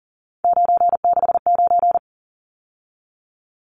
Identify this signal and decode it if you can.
Morse code, heard as a keyed tone, about 40 wpm, 700 Hz. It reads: 969
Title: Numbers - 3-Digit 40wpm